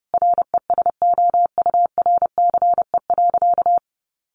Morse code, heard as a keyed tone, about 30 wpm, 700 Hz. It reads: RESOURCE.
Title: Sentences from Top 1300 Words 30wpm